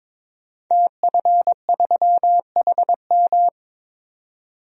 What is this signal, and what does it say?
Morse code, heard as a keyed tone, about 22 wpm, 700 Hz. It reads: TF3HM
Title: DXCC-100 Call Signs - 5 Characters - 22wpm